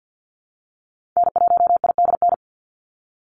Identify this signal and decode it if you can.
Morse code, heard as a keyed tone, about 50 wpm, 700 Hz. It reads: N1IDN
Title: NA SOTA Activator Callsigns 50wpm